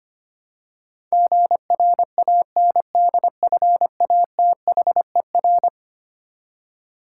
Morse code, heard as a keyed tone, about 25 wpm, 700 Hz. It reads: GRANDFATHER
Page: morse-code-ninja-practice-25wpm